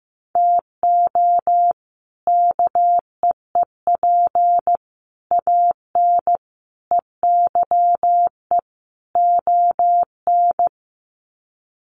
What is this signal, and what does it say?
Morse code, heard as a keyed tone, about 15 wpm, 700 Hz. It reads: TO KEEP AN EYE ON
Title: Sets of 5 Words Encore 15wpm